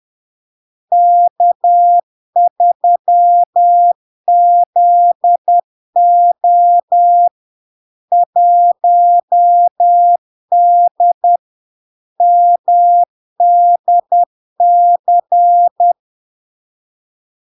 Morse code, heard as a keyed tone, about 10 wpm, 700 Hz. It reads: K3ZO 1D MDC
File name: LARK-FD-2022-Log-in-CW-10wpm-700-Hz